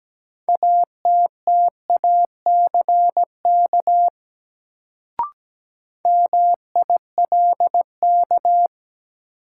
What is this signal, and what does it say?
Morse code, heard as a keyed tone, about 17 wpm, 700 Hz. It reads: ATTACK MILK
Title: Sentences from Top 1500 Words - New Word Drill - 17wpm